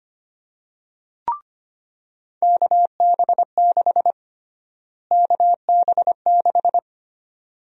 Morse code, heard as a keyed tone, about 25 wpm, 700 Hz. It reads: KB6 KB6
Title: USA Call Sign Prefixes - Repeated 2x - 25wpm